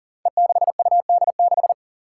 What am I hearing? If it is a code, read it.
Morse code, heard as a keyed tone, about 37 wpm, 700 Hz. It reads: E6UD6
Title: Five Challenging Letter-Number Combinations 40wpm